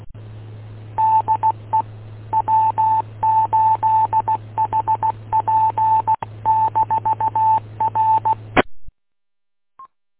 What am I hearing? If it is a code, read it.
Morse code, heard as a keyed tone, about 16 wpm, 880 Hz. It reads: DE W8HP-R